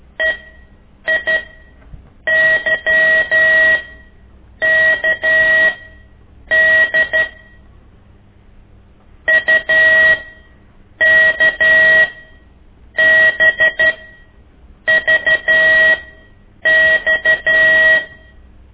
Morse code, heard as a keyed tone, about 13 wpm, spaced as for 7 wpm, 1900 Hz. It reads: EIYKD UKBVX
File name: ik1weg2 / CORSI CW